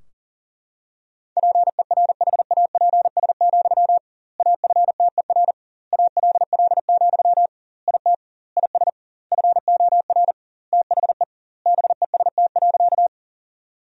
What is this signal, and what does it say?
Morse code, heard as a keyed tone, about 40 wpm, 700 Hz. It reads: PERHAPS, AFTER ALL, IT IS FOR THE BEST.
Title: 2x Word-Spacing - Sentences from Top 500 Words - 40wpm